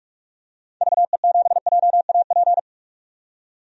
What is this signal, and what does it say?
Morse code, heard as a keyed tone, about 45 wpm, 690 Hz. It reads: VE7JAP
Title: DXCC-100 Call Signs - 6 Characters - 45wpm